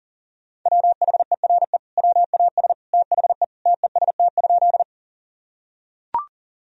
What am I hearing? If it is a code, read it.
Morse code, heard as a keyed tone, about 40 wpm, 700 Hz. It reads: WHERE WAS THE TEST?
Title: Sentences from Top 1000 Words - Review Part 3 - 40wpm